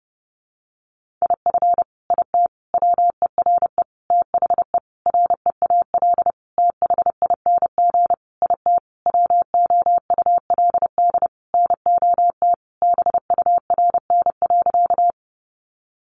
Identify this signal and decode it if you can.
Morse code, heard as a keyed tone, about 30 wpm, 700 Hz. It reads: IF IT WERE THE REAL THING IT WOULD NOT BURN.
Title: Sentences from Top 1300 Words 30wpm